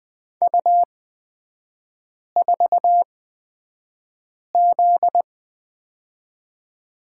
Morse code, heard as a keyed tone, about 20 wpm, spaced as for 5 wpm, 700 Hz. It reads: U4Z
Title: 243 Lesson 39e - Letter z - Three Characters 20wpm